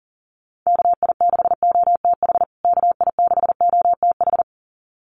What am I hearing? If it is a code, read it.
Morse code, heard as a keyed tone, about 40 wpm, 700 Hz. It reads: KI6OTH KI6OTH
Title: USA Call Signs Repeated 2x 40wpm